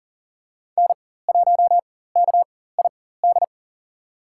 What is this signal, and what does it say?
Morse code, heard as a keyed tone, about 40 wpm, spaced as for 19 wpm, 700 Hz. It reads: N1KID
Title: USA Call Signs - 5 Characters - 4x Character Spacing - 40wpm